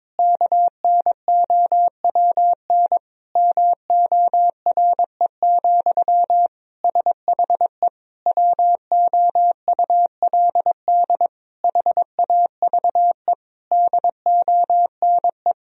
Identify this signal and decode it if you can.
Morse code, heard as a keyed tone, about 22 wpm, 700 Hz. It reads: KNOWN MORE, SHE WOULD HAVE DONE
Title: Sentences from Top 1000 Words - Review Part 3 - 22wpm